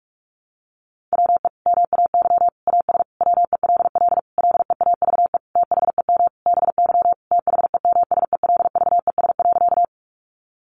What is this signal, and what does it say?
Morse code, heard as a keyed tone, about 44 wpm, 700 Hz. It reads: WE MAY AS WELL LEAVE THEM BY THEMSELVES.
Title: Sentences from Top 800 Words 45wpm